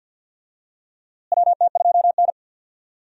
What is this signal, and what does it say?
Morse code, heard as a keyed tone, about 46 wpm, 700 Hz. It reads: WT2N
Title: USA Call Signs - 4 Characters - Repeated 2x - 50wpm